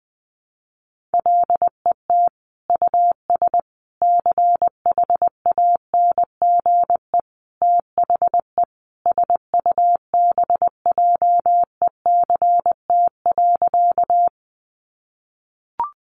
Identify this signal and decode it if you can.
Morse code, heard as a keyed tone, about 19 wpm, 700 Hz. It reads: LET US CHANGE THE SUBJECT.
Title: Sentences from Top 800 Words 20wpm